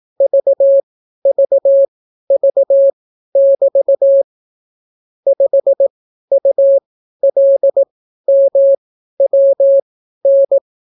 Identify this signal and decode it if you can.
Morse code, heard as a keyed tone, about 18 wpm, spaced as for 12 wpm, 550 Hz. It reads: VVV<BT> 5ULMWN